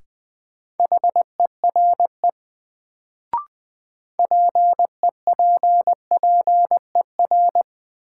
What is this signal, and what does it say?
Morse code, heard as a keyed tone, about 21 wpm, 700 Hz. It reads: HERE PEPPER
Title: Sets of 5 Words - New Word Drill - 20wpm